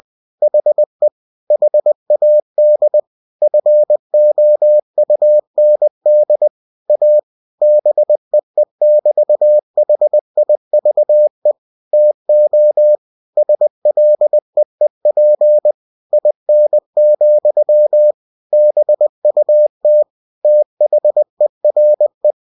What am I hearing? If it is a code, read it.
Morse code, heard as a keyed tone, about 20 wpm, 600 Hz. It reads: HE HAD FOUND A BEE<BT>HIVE TO SLEEP IN, BUT THERE
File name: cw-books / winnie-the-pooh / winnie-20wpm-page-0093